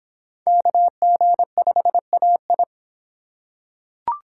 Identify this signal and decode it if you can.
Morse code, heard as a keyed tone, about 24 wpm, 700 Hz. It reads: KG5AI